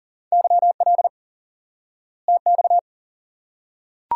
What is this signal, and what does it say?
Morse code, heard as a keyed tone, about 40 wpm, 700 Hz. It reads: YL TX
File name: IQE Course 3 - Lesson 89 - YAGI - Two Elements - 40wpm